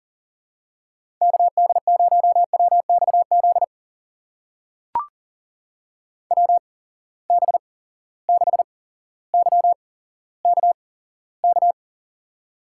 Morse code, heard as a keyed tone, about 39 wpm, spaced as for 17 wpm, 700 Hz. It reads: KD0WXZ WB6YKK
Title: USA Call Signs - 6 Characters - 8x Character Spacing - 40wpm